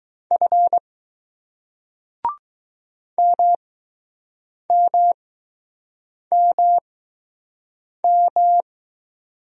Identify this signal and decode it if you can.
Morse code, heard as a keyed tone, about 20 wpm, spaced as for 6 wpm, 700 Hz. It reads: F MMMM